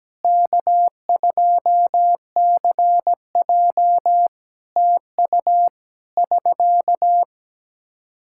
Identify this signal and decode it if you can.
Morse code, heard as a keyed tone, about 17 wpm, 700 Hz. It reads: K2CJ TU <SK>
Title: K1USN SST QSO Practice 17wpm